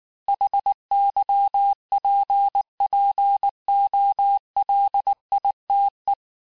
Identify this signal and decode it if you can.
Morse code, heard as a keyed tone, about 19 wpm, 800 Hz. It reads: HYPPOLITE
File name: cours-du-lundi files / cours-1589-texte-clair-vitesse-18mpm-puis-23mpm